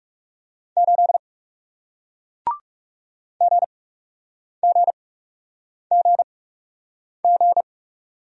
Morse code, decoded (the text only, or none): 8 GGGG